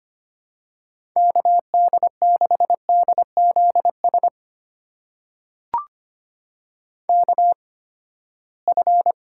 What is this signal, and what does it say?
Morse code, heard as a keyed tone, about 25 wpm, spaced as for 17 wpm, 700 Hz. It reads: KD6DZS KF
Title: USA Call Signs - 6 Characters - 8x Character Spacing - 25wpm